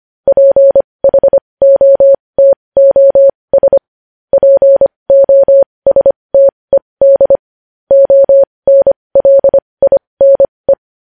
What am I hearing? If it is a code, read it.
Morse code, heard as a keyed tone, about 25 wpm, spaced as for 20 wpm, 550 Hz. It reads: PHOTOS POSTED ONLINE
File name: podcast / News headlines-1x-25x20